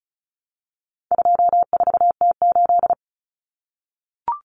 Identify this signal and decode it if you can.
Morse code, heard as a keyed tone, about 34 wpm, 700 Hz. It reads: 24T8
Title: Common Cut Numbers - 4-Digit 35wpm